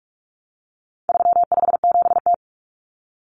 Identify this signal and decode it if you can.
Morse code, heard as a keyed tone, about 44 wpm, 700 Hz. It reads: 357T
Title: Common Cut Numbers - 4-Digit 45wpm